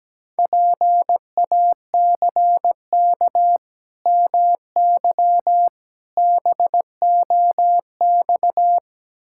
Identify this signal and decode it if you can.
Morse code, heard as a keyed tone, about 17 wpm, 700 Hz. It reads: PACK MY BOX
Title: Pangram Challenge v2 - 17wpm